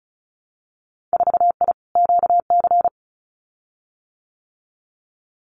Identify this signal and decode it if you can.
Morse code, heard as a keyed tone, about 34 wpm, 700 Hz. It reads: 4I QC
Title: Winter Field Day Exchanges Encore 35wpm